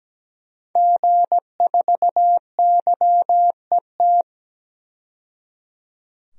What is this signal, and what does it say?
Morse code, heard as a keyed tone, about 17 wpm, 700 Hz. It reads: G4YET
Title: DXCC-100 Call Signs - 5 Characters - 17wpm